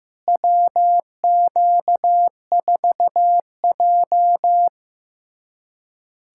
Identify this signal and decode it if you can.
Morse code, heard as a keyed tone, about 16 wpm, 700 Hz. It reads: WQ4J
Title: USA Callsign DE Callsign - 4 Characters - 15wpm